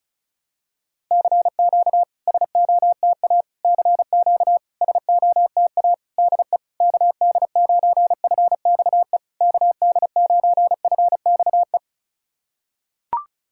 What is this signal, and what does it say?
Morse code, heard as a keyed tone, about 36 wpm, 690 Hz. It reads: CQ SOTA CQ SOTA DE KD9FXE KD9FXE
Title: SOTA QSO Practice 35wpm